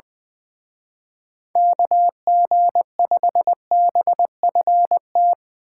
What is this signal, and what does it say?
Morse code, heard as a keyed tone, about 23 wpm, 700 Hz. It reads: KG5BFT
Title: Speed Racing - USA 2x3 Call Signs 20wpm